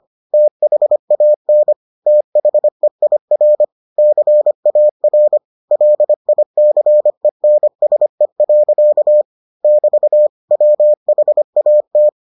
Morse code, heard as a keyed tone, about 25 wpm, 600 Hz. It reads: THAN THEIR CAR LICENSE. <BT> WHAT